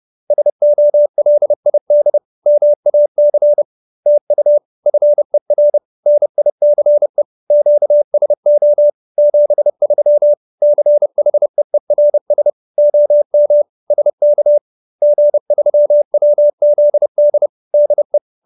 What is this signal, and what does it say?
Morse code, heard as a keyed tone, about 30 wpm, 590 Hz. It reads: SOLID MAC TU FER NICE QSO 73 CHEERS OM SK G3WZD DE
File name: QSO310 30